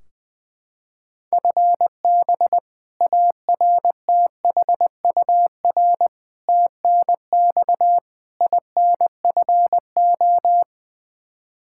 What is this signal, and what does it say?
Morse code, heard as a keyed tone, about 20 wpm, 700 Hz. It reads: FB ARTHUR TNX INFO
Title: Practice QSO - Exchange 3 - More Info - 20wpm